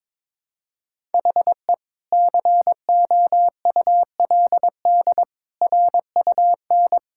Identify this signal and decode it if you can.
Morse code, heard as a keyed tone, about 22 wpm, 700 Hz. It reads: HE COULD RUN
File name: practice / 22wpm / Sentences with 2-Word Phrases - 22wpm